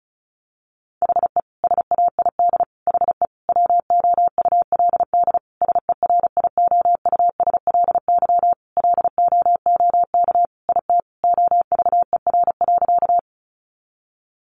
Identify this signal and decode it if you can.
Morse code, heard as a keyed tone, about 34 wpm, 700 Hz. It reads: HE SAID HE WOULD SERIOUSLY LOOK IT OVER.
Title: Sentences from Top 1200 Words 35wpm